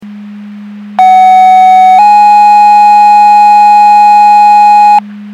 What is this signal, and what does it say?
Two-tone page, heard as a one-second tone, then a three-second tone, about 750 then 830 Hz. It is heard from 1.0 to 5.0 s.
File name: Portage County 154.3850 Fire EMS Paging